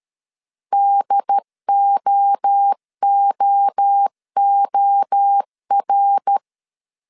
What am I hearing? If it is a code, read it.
Morse code, heard as a keyed tone, about 13 wpm, 800 Hz. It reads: DOOOR
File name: CW / DET / 1 3 1500C S C 12wpm 2750